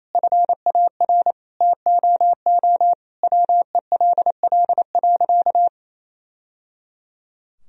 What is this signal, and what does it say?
Morse code, heard as a keyed tone, about 28 wpm, 700 Hz. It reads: FAR TOO WELL.
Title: Sentences from Top 1000 Words - Review Part 3 - 28wpm